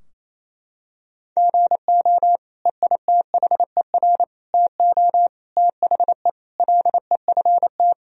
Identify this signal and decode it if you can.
Morse code, heard as a keyed tone, about 28 wpm, 700 Hz. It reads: GO EITHER TO THE LEFT
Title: Sentences from Top 1000 Words - Review Part 4 - 28wpm